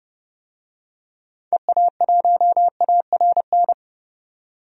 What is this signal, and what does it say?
Morse code, heard as a keyed tone, about 30 wpm, 700 Hz. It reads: EA1ARN
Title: DXCC-100 Call Signs - 6 Characters - 30wpm